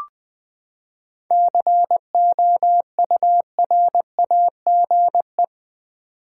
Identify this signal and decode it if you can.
Morse code, heard as a keyed tone, about 20 wpm, 700 Hz. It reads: COURAGE
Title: Seven Letter Words - 20wpm